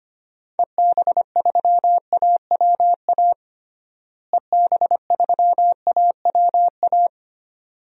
Morse code, heard as a keyed tone, about 25 wpm, 700 Hz. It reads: EB3AWA EB3AWA
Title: DXCC-100 Call Signs - 6 Characters - Repeated 2x - 25wpm